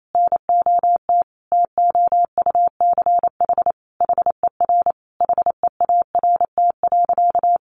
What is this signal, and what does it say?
Morse code, heard as a keyed tone, about 28 wpm, 700 Hz. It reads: NOT TOUCH HER HEART.
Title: Sentences from Top 1200 Words 28wpm